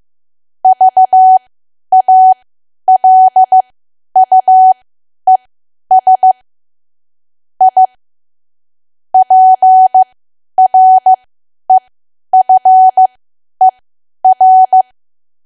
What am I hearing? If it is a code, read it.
Morse code, heard as a keyed tone, about 15 wpm, spaced as for 10 wpm, 750 Hz. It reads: VALUES I PREFER
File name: cwa / W1AW-Level2 / 10WPM